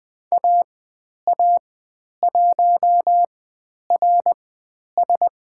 Morse code, heard as a keyed tone, about 20 wpm, spaced as for 10 wpm, 700 Hz. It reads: AA1RS
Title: 109 Lesson 18g - Letter m - Call Signs 20wpm